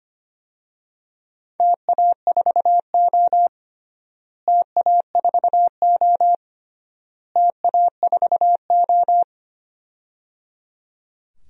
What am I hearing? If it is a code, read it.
Morse code, heard as a keyed tone, about 25 wpm, 700 Hz. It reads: TA4O TA4O TA4O
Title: DXCC-100 Call Signs - 4 Characters - Repeated 3x - 25wpm